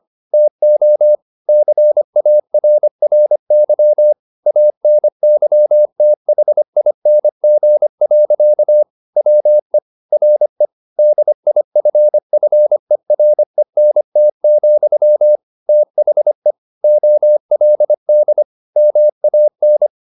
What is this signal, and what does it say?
Morse code, heard as a keyed tone, about 25 wpm, 600 Hz. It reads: TO CARRY ANYTHING. WE RE DIFFERENT, THE OLD MAN
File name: the old man and the sea part 1 25wpm